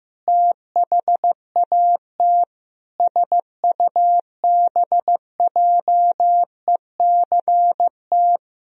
Morse code, heard as a keyed tone, about 16 wpm, 700 Hz. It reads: THAT SUBJECT
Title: Sentences from Top 1000 Words - Review Part 2 - 15wpm